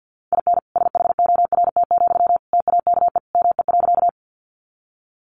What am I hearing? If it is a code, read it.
Morse code, heard as a keyed tone, about 50 wpm, 700 Hz. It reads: IN SHORT, TAKE ME.
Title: Sentences from Top 400 Words 50wpm